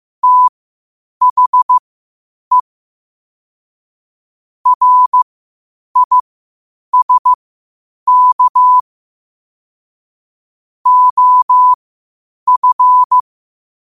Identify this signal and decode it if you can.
Morse code, heard as a keyed tone, about 15 wpm, spaced as for 8 wpm, 1000 Hz. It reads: THE RISK OF
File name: qotdfiles / 8wpm